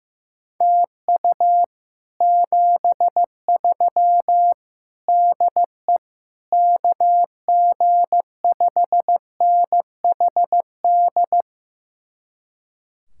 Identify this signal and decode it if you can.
Morse code, heard as a keyed tone, about 15 wpm, 700 Hz. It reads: TU 73 DE KG5NHD
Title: Speed-Racing Common QSO Phrases Encore - 15wpm